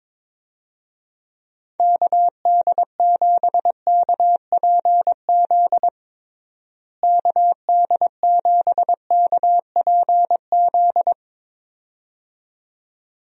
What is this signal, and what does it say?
Morse code, heard as a keyed tone, about 22 wpm, 700 Hz. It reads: KD7KPZ KD7KPZ
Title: USA Call Signs - 6 Characters - Repeated 2x - 22wpm